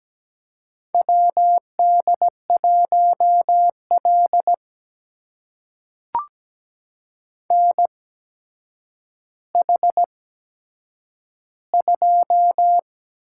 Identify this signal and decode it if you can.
Morse code, heard as a keyed tone, about 17 wpm, spaced as for 8 wpm, 700 Hz. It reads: WD1L NH2
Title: USA Call Signs - 4 Characters - 8x Character Spacing - 17wpm